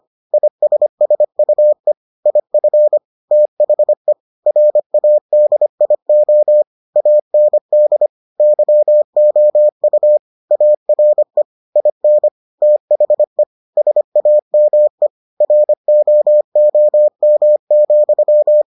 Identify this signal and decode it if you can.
Morse code, heard as a keyed tone, about 25 wpm, 600 Hz. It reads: ISSUE IF THE RADIO AND YOU ARE IN THE SAME ROOM,